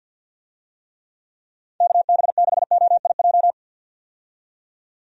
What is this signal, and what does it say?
Morse code, heard as a keyed tone, about 48 wpm, 700 Hz. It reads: KB6OIJ
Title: USA Call Signs - 6 Characters - 50wpm